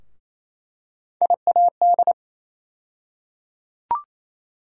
Morse code, heard as a keyed tone, about 27 wpm, 700 Hz. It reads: IAD
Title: USA Call Sign Postfixes - 3 Letters 28wpm